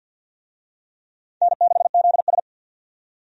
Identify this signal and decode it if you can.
Morse code, heard as a keyed tone, about 51 wpm, 700 Hz. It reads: N6ZS
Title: USA Call Signs - 4 Characters - 50wpm